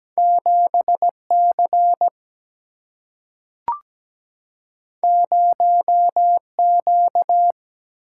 Morse code, heard as a keyed tone, about 17 wpm, 700 Hz. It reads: 7C 0Q
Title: Single-Letter Single-Number 17wpm